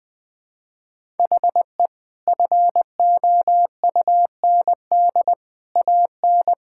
Morse code, heard as a keyed tone, about 20 wpm, 700 Hz. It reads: HE FOUND AN
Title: Sentences from Top 1000 Words - Review Part 6 - 20wpm